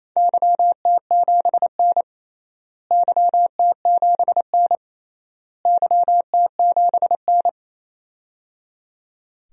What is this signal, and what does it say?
Morse code, heard as a keyed tone, about 28 wpm, 700 Hz. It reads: YT7N YT7N YT7N
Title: DXCC-100 Call Signs - 4 Characters - Repeated 3x - 28wpm